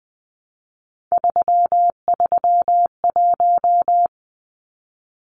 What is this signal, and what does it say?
Morse code, heard as a keyed tone, about 20 wpm, 700 Hz. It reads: 331
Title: Numbers - 3-Digit 20wpm